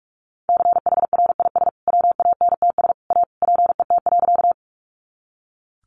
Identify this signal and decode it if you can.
Morse code, heard as a keyed tone, about 44 wpm, 700 Hz. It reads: CHRIS WANTS A PET.
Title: Easy Phrases 45wpm